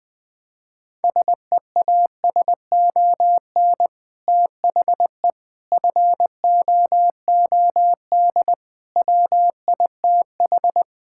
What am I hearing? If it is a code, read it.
Morse code, heard as a keyed tone, about 20 wpm, 700 Hz. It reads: SEASON THE FOOD WITH
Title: Sentences with 5-Word Phrases - 20wpm